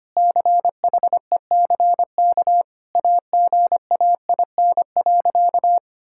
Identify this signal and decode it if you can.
Morse code, heard as a keyed tone, about 25 wpm, 700 Hz. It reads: CHECK AGAIN.
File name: practice / 25wpm / Sentences with 4-Word Phrases - 25wpm